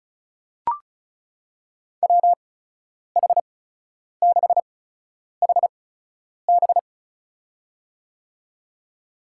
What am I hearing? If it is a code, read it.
Morse code, heard as a keyed tone, about 34 wpm, spaced as for 10 wpm, 700 Hz. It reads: WH6HB
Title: USA Call Signs - 5 Characters - 8x Character Spacing - 35wpm